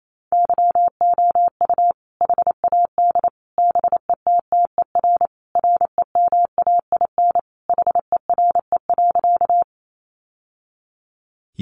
Morse code, heard as a keyed tone, about 28 wpm, 700 Hz. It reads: YOU HAD BETTER REMAIN HERE.
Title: Sentences from Top 1200 Words 28wpm